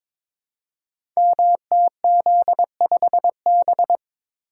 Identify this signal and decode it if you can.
Morse code, heard as a keyed tone, about 22 wpm, 700 Hz. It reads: MTZ5B
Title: Code Groups - 5 Letter:Numbers - 22wpm